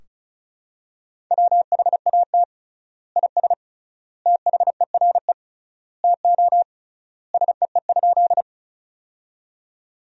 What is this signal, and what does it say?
Morse code, heard as a keyed tone, about 36 wpm, 700 Hz. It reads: WHAT IS THERE TO SEE?
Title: 3x Word-Spacing - Sentences from Top 100 Words - 35wpm